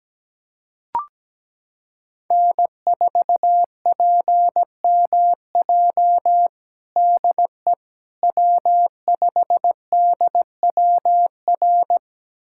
Morse code, heard as a keyed tone, about 17 wpm, 700 Hz. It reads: N4PMJ DE W5DWR
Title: USA Callsign DE Callsign - 5 Characters - 17wpm